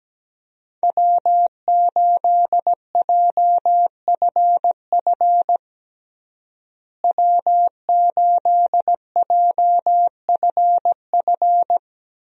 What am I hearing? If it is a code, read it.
Morse code, heard as a keyed tone, about 17 wpm, 700 Hz. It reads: W8JFF W8JFF
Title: USA Call Signs - 5 Characters - Repeated 3x - 17wpm